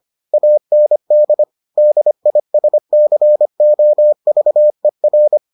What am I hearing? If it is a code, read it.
Morse code, heard as a keyed tone, about 25 wpm, 600 Hz. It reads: AND DISCOVER